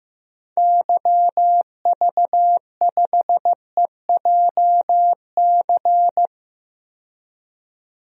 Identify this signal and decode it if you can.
Morse code, heard as a keyed tone, about 15 wpm, 700 Hz. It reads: YV5EJC
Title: DXCC-100 Call Signs - 6 Characters - 15wpm